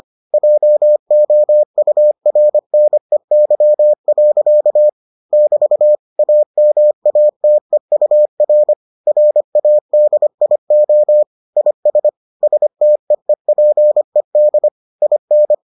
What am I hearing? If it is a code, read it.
Morse code, heard as a keyed tone, about 25 wpm, 600 Hz. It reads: JOURNEY. <BT> AMATEUR RADIO IS STEEPED IN